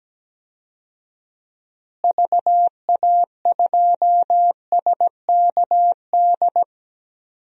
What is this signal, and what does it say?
Morse code, heard as a keyed tone, about 17 wpm, 700 Hz. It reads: VA2SKD
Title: DXCC-100 Call Signs - 6 Characters - Repeated 2x - 17wpm